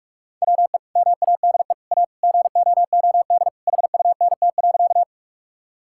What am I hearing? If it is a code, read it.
Morse code, heard as a keyed tone, about 41 wpm, 700 Hz. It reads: WE MADE A GOOD HUNT.
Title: Sentences from Top 1600 Words 45wpm